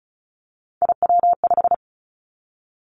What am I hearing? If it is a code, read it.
Morse code, heard as a keyed tone, about 36 wpm, 700 Hz. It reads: IW5
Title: Three Challenging Letter-Number Combinations 35wpm